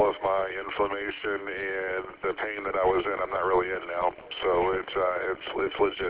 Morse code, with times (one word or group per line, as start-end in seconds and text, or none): none